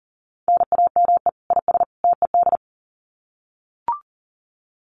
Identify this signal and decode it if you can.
Morse code, heard as a keyed tone, about 40 wpm, 700 Hz. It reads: NAME IS TED
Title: Common QSO Phrases 40wpm